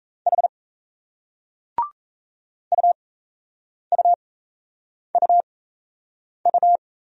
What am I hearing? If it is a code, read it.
Morse code, heard as a keyed tone, about 35 wpm, spaced as for 8 wpm, 700 Hz. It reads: H UUUU
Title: Speed-Racing Single Letters 28wpm